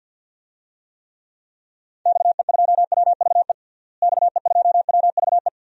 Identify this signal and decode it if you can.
Morse code, heard as a keyed tone, about 47 wpm, 700 Hz. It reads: XE2WVE XE2WVE
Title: DXCC-100 Call Signs - 6 Characters - Repeated 2x - 50wpm